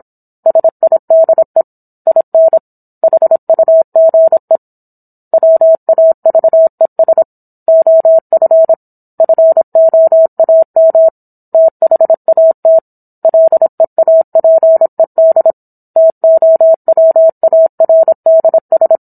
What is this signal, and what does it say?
Morse code, heard as a keyed tone, about 27 wpm, 650 Hz. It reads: SIDE IN HUGE WAVES OF FOAM THAT LEAPED TOWARDS